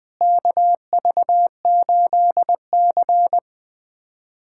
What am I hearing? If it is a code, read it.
Morse code, heard as a keyed tone, about 20 wpm, 700 Hz. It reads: KV8C
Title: USA Callsign DE Callsign - 4 Characters - 20wpm